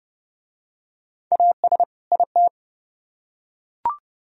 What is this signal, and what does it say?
Morse code, heard as a keyed tone, about 28 wpm, 700 Hz. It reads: AS IT